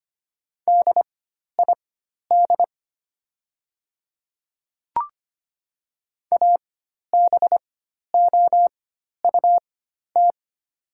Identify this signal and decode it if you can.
Morse code, heard as a keyed tone, about 25 wpm, spaced as for 12 wpm, 700 Hz. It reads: DID ABOUT
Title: 4x Character Spacing - Top 100 Words - 25wpm